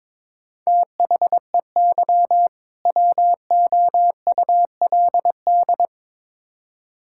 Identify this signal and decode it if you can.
Morse code, heard as a keyed tone, about 22 wpm, 700 Hz. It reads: THEY WOULD
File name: Build A Phrase - 2-Word Phrases - 22wpm